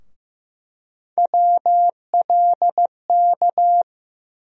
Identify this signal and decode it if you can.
Morse code, heard as a keyed tone, about 15 wpm, 700 Hz. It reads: WLK
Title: USA Call Sign Postfixes - 3 Letters 15wpm